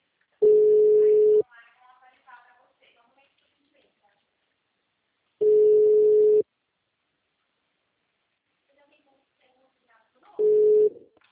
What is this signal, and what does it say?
Ringback tone (Europe, 425 Hz), with speech.
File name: out- -2010- - - .13463